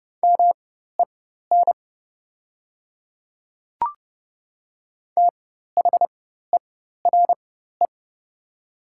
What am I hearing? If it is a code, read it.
Morse code, heard as a keyed tone, about 30 wpm, spaced as for 14 wpm, 700 Hz. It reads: MEN THERE
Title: 4x Character Spacing - Top 200 Words - 30wpm